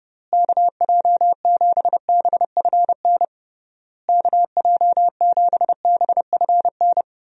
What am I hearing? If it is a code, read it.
Morse code, heard as a keyed tone, about 32 wpm, 700 Hz. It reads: KJ7BFN KJ7BFN
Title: USA Call Signs - 6 Characters - Repeated 3x - 30wpm